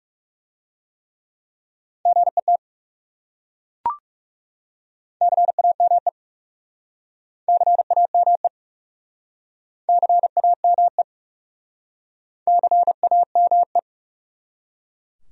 Morse code, heard as a keyed tone, about 37 wpm, 700 Hz. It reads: GET CAME CAME CAME CAME
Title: Speed-Racing Top 200 Words 30wpm